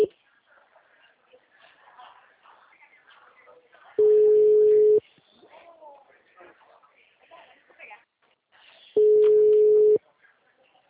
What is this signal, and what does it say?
Ringback tone (Europe, 425 Hz), with speech.